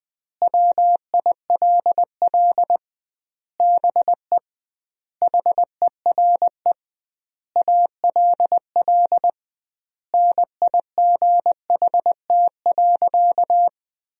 Morse code, heard as a keyed tone, about 20 wpm, 700 Hz. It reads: WILL BE HERE ALL NIGHT.
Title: 2x Word-Spacing - Sentences from Top 300 Words - 20wpm